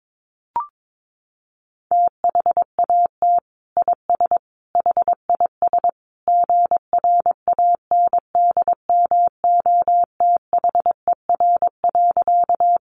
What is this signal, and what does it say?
Morse code, heard as a keyed tone, about 22 wpm, 700 Hz. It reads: THAT IS HIS GRANDMOTHER.
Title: Sentences from Top 1200 Words 22wpm